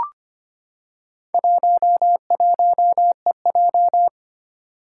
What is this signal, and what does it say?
Morse code, heard as a keyed tone, about 25 wpm, 700 Hz. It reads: 11EJ
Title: Four Challenging Letter-Number Combinations 25wpm